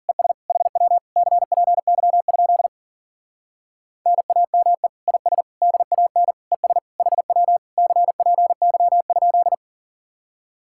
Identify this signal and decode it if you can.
Morse code, heard as a keyed tone, about 52 wpm, 700 Hz. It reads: ES HW CPY? NAME IS DAN ES HW CPY?